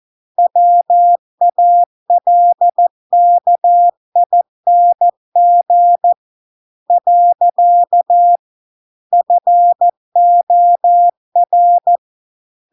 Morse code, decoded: WALKING . FOR

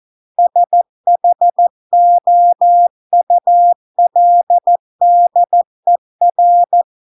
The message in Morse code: SHOULDER

